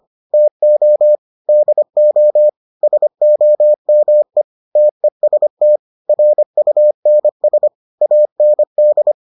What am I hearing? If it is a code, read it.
Morse code, heard as a keyed tone, about 25 wpm, 600 Hz. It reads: TO DO SOME TEST RUNS AND